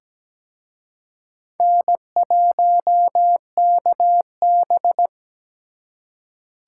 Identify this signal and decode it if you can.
Morse code, heard as a keyed tone, about 20 wpm, 700 Hz. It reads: N1KB
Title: Speed Racing - USA 1x2 Call Signs 17wpm